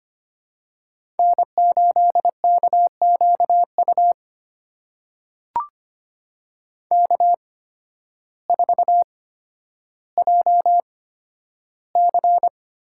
Morse code, heard as a keyed tone, about 25 wpm, spaced as for 12 wpm, 700 Hz. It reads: N8KQU K4JC
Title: USA Call Signs - 5 Characters - 8x Character Spacing - 25wpm